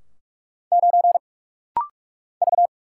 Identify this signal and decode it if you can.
Morse code, heard as a keyed tone, about 44 wpm, 690 Hz. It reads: 9 V
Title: Single Letter-Number - Rapid-Fire 45wpm